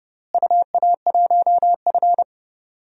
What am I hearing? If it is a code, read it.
Morse code, heard as a keyed tone, about 30 wpm, 700 Hz. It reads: UA1F
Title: DXCC-100 Call Signs - 4 Characters - Repeated 2x - 30wpm